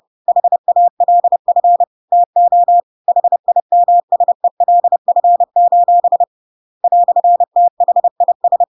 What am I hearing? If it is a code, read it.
Morse code, heard as a keyed tone, about 30 wpm, 700 Hz. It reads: HALF TO HIMSELF: "THIS
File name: Pinocchio / Pinocchio ch1 30 30 ik1hge